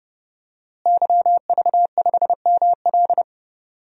Morse code, heard as a keyed tone, about 30 wpm, 700 Hz. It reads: YV5ML
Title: DXCC-100 Call Signs - 5 Characters - Repeated 3x - 30wpm